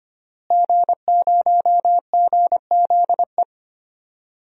Morse code, heard as a keyed tone, about 25 wpm, 700 Hz. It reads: G0GZE
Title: DXCC-100 Call Signs - 5 Characters - Repeated 3x - 25wpm